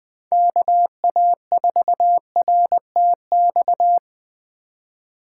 Morse code, heard as a keyed tone, about 20 wpm, 700 Hz. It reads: KA4RTX